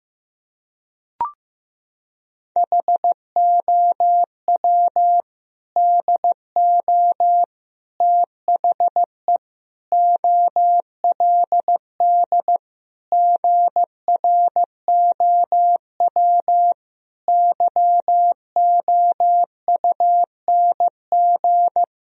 Morse code, heard as a keyed tone, about 15 wpm, 700 Hz. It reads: HOW DO THE OLD GROW YOUNG
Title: Sentences from Top 1000 Words - Review Part 1 - 15wpm